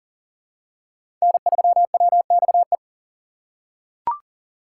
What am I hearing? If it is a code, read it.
Morse code, heard as a keyed tone, about 37 wpm, 700 Hz. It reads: N3WXE
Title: USA Call Signs - 5 Characters - 6x Character Spacing - 40wpm